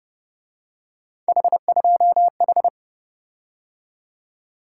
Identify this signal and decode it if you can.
Morse code, heard as a keyed tone, about 29 wpm, 700 Hz. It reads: H2H
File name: Speed-Racing - Three Letter-Number Combinations - 25wpm